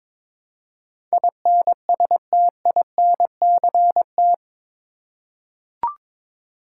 Morse code, heard as a keyed tone, about 22 wpm, 700 Hz. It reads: INSTINCT
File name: practice / 22wpm / Eight Letter Words - 22wpm